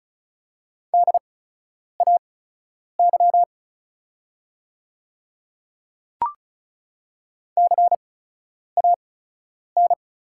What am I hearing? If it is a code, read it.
Morse code, heard as a keyed tone, about 34 wpm, spaced as for 10 wpm, 700 Hz. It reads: DAY CAN
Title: 8x Character Spacing - Top 100 Words - 35wpm